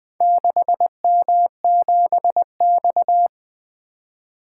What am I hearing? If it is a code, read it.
Morse code, heard as a keyed tone, about 20 wpm, 700 Hz. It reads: 6M7X
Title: Four Letter-Number Combinations - 20wpm